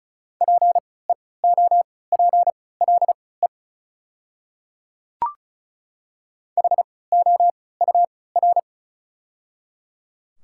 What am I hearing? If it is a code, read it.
Morse code, heard as a keyed tone, about 36 wpm, spaced as for 20 wpm, 700 Hz. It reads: PEOPLE HOUR
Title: 3x Character Spacing - Top 500 Words - 35wpm